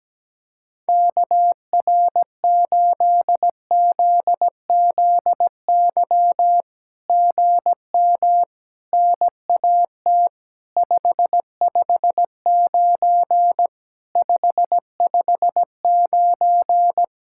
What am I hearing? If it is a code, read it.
Morse code, heard as a keyed tone, about 17 wpm, 700 Hz. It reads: KR8ZZY GM NAT 559 559
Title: SOTA QSO Practice 17wpm